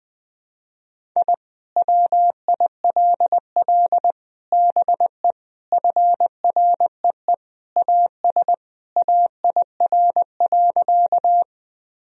Morse code, heard as a keyed tone, about 20 wpm, 700 Hz. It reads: I WILL BE FREE AS AIR.
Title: Sentences from Top 1000 Words - Review Part 1 - 20wpm